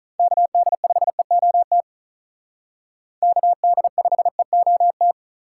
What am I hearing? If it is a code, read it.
Morse code, heard as a keyed tone, about 45 wpm, 700 Hz. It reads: KD5EOT KD5EOT